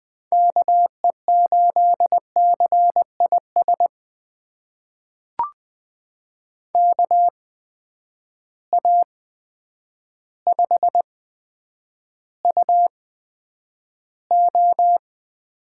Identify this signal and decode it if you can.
Morse code, heard as a keyed tone, about 20 wpm, spaced as for 9 wpm, 700 Hz. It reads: KE8CIS KA5UO
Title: USA Call Signs - 6 Characters - 8x Character Spacing - 20wpm